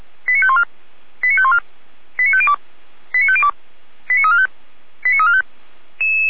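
Five-tone selective call (ZVEI), heard as quick runs of five tones, tones of about 70 ms. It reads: 89514 89514 89502 89502 89345 89345
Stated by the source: ZVEI five-tone selective call received on an SDR.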